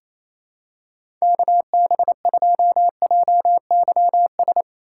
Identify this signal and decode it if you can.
Morse code, heard as a keyed tone, about 28 wpm, 700 Hz. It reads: KB2JYS
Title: USA Call Signs - 6 Characters - 4x Character Spacing - 28wpm